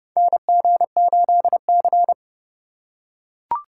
Morse code, heard as a keyed tone, about 31 wpm, 700 Hz. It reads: NG8C
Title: USA Call Signs - 4 Characters - 2x Character Spacing - 30wpm